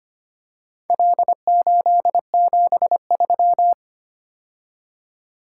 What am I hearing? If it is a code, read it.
Morse code, heard as a keyed tone, about 25 wpm, 700 Hz. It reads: L873
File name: practice / 25wpm / Four Letter-Number Combinations - 25wpm